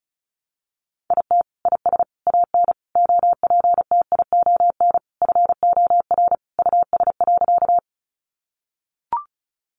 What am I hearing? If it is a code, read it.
Morse code, heard as a keyed tone, about 34 wpm, 700 Hz. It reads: IT IS AN OPTION FOR US.
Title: Sentences from Top 1300 Words 35wpm